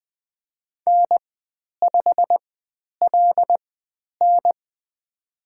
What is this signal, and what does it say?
Morse code, heard as a keyed tone, about 20 wpm, spaced as for 10 wpm, 700 Hz. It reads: N5LN
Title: 109 Lesson 18g - Letter m - Call Signs 20wpm